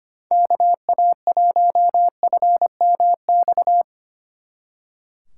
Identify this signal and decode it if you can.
Morse code, heard as a keyed tone, about 31 wpm, 700 Hz. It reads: KA1FMX